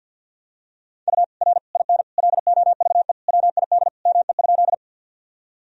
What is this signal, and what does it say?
Morse code, heard as a keyed tone, about 51 wpm, 700 Hz. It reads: U R IN LOVE WID ME?
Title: Sentences from Top 400 Words and Abbreviations 50wpm